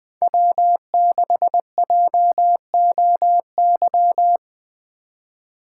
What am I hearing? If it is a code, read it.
Morse code, heard as a keyed tone, about 24 wpm, 700 Hz. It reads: W6JOY
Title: Speed Racing - USA 1x3 Call Signs 20wpm